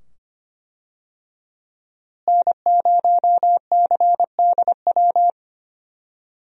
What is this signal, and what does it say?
Morse code, heard as a keyed tone, about 25 wpm, 700 Hz. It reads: N0CDW
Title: USA Call Signs - 5 Characters - Repeated 2x - 25wpm